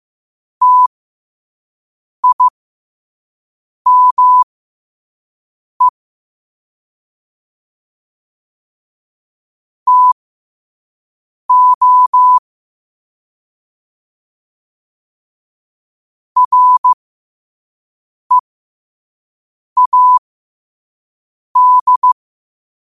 Morse code, heard as a keyed tone, about 15 wpm, spaced as for 5 wpm, 1000 Hz. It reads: TIME TO READ
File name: qotdfiles / 5wpm